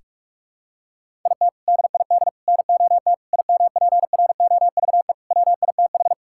Morse code, heard as a keyed tone, about 46 wpm, 700 Hz. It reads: IT DID NOT IMPROVE WITH
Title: Sentences from Top 1000 Words - Review Part 5 - 45wpm